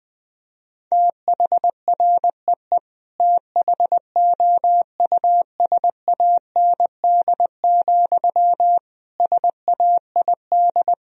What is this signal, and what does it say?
Morse code, heard as a keyed tone, about 21 wpm, 700 Hz. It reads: THREE THOUSAND, SAID
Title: Sentences from Top 1000 Words - Review Part 3 - 20wpm